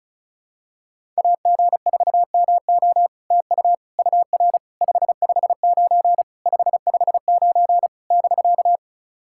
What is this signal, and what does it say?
Morse code, heard as a keyed tone, about 36 wpm, 700 Hz. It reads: AG4MO TU UR 559 559 <BK>
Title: SOTA QSO Practice 35wpm